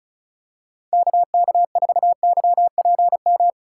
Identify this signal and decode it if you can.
Morse code, heard as a keyed tone, about 34 wpm, 700 Hz. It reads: KK4YPM
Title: USA Call Signs - 6 Characters - 4x Character Spacing - 35wpm